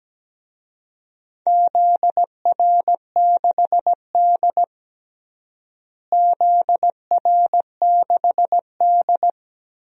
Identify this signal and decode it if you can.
Morse code, heard as a keyed tone, about 17 wpm, 700 Hz. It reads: ZR6D ZR6D
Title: DXCC-100 Call Signs - 4 Characters - Repeated 2x - 17wpm